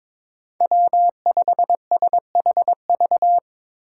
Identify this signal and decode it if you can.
Morse code, heard as a keyed tone, about 22 wpm, 700 Hz. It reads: W5SHV
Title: USA Call Signs - 5 Characters - 2x Character Spacing - 22wpm